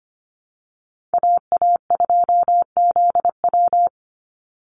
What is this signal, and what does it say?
Morse code, heard as a keyed tone, about 25 wpm, 700 Hz. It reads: AA2ZW
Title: Call Signs Repeated 2x - 25wpm